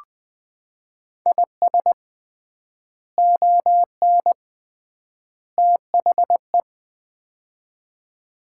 Morse code, heard as a keyed tone, about 20 wpm, 700 Hz. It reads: IS ON THE
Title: 3x Word-Spacing - Sets of 3 Words - 20wpm